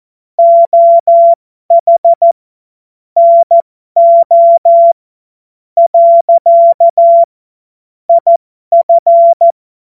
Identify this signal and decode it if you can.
Morse code, heard as a keyed tone, about 14 wpm, spaced as for 10 wpm, 680 Hz. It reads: OH NO. IF
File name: Morsekurs / Witze